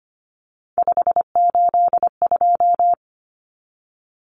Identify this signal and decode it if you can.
Morse code, heard as a keyed tone, about 25 wpm, 700 Hz. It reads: 582
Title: Numbers - 3-Digit 25wpm